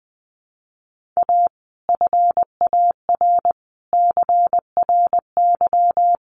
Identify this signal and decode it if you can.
Morse code, heard as a keyed tone, about 20 wpm, 700 Hz. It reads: A FAR CRY
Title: Sets of 5 Words Encore 20wpm